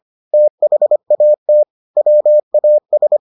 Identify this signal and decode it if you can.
Morse code, heard as a keyed tone, about 25 wpm, 600 Hz. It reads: THAT WAS